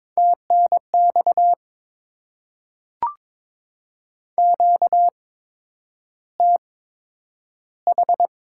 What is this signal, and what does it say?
Morse code, heard as a keyed tone, about 22 wpm, spaced as for 9 wpm, 700 Hz. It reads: TNX QTH
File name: Top 20 QSO Elements - 8x Character Spacing - 22wpm